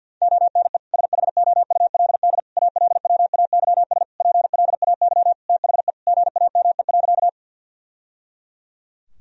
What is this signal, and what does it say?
Morse code, heard as a keyed tone, about 51 wpm, 700 Hz. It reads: ONE SHOULD ALWAYS PLAY THE GAME.
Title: Sentences from Top 1000 Words - Review Part 3 - 50wpm